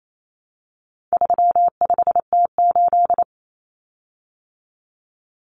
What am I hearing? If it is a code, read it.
Morse code, heard as a keyed tone, about 28 wpm, 700 Hz. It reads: 35T8
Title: Common Cut Numbers - 4-Digit 28wpm